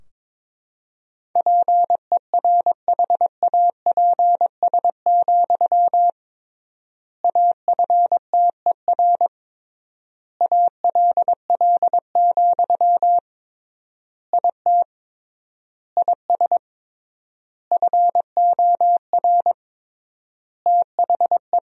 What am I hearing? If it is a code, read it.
Morse code, heard as a keyed tone, about 22 wpm, 700 Hz. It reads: PERHAPS, AFTER ALL, IT IS FOR THE
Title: 3x Word-Spacing - Sentences from Top 500 Words - 22wpm